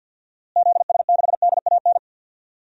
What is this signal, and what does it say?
Morse code, heard as a keyed tone, about 47 wpm, 700 Hz. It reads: ZS6DAN
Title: DXCC-100 Call Signs - 6 Characters - Repeated 3x - 50wpm